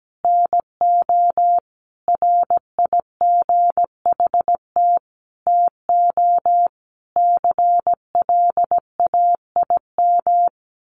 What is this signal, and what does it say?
Morse code, heard as a keyed tone, about 17 wpm, 700 Hz. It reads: NO RIGHT TO CLAIM
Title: Sentences from Top 1500 Words 17wpm